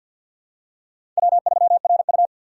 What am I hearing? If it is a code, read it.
Morse code, heard as a keyed tone, about 53 wpm, 700 Hz. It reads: W3RU
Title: USA Call Signs - 4x Character Spacing - 50wpm